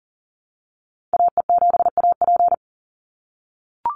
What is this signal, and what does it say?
Morse code, heard as a keyed tone, about 40 wpm, 700 Hz. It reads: AE7AP
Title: NA SOTA Activator Callsigns 40wpm